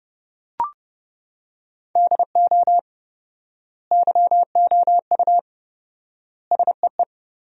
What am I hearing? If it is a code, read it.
Morse code, heard as a keyed tone, about 30 wpm, 700 Hz. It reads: DO YOU SEE
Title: 4x Word-Spacing - Sentences from Top 100 Words - 30wpm